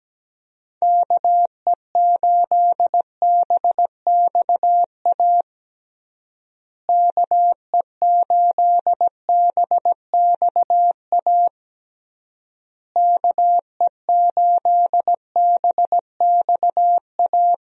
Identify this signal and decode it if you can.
Morse code, heard as a keyed tone, about 17 wpm, 700 Hz. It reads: KE8BXA KE8BXA KE8BXA
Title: USA Call Signs - 6 Characters - Repeated 3x - 17wpm